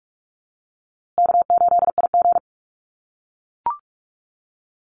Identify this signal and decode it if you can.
Morse code, heard as a keyed tone, about 44 wpm, 700 Hz. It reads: K8IG
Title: USA Call Signs Repeated 3x 45wpm